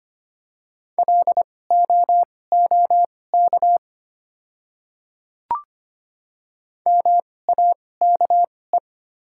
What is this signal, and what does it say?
Morse code, heard as a keyed tone, about 25 wpm, spaced as for 18 wpm, 690 Hz. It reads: LOOK MAKE
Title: 2x Character Spacing - Top 100 Words - 25wpm